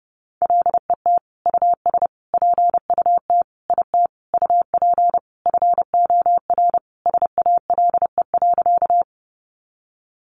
Morse code, heard as a keyed tone, about 30 wpm, 700 Hz. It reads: LET US PUT IT UP FOR SALE.
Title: Sentences from Top 1400 Words 30wpm